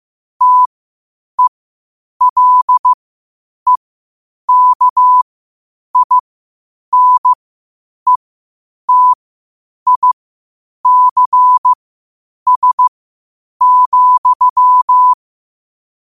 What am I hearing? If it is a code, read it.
Morse code, heard as a keyed tone, about 15 wpm, spaced as for 8 wpm, 1000 Hz. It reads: TELEKINETICS,
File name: qotdfiles / 8wpm